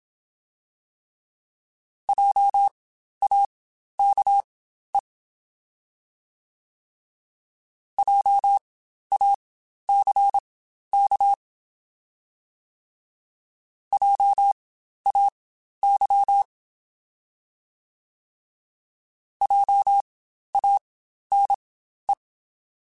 Morse code, heard as a keyed tone, about 27 wpm, spaced as for 12 wpm, 800 Hz. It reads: JAKE JACK JAY JANE
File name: Spaced Alphas / Spaced Alphas mp3s / Js sp @25